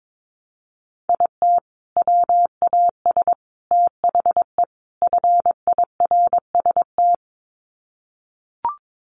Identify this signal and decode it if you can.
Morse code, heard as a keyed tone, about 22 wpm, 700 Hz. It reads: IT WAS THE FIRST
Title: Sets of 4 Words 22wpm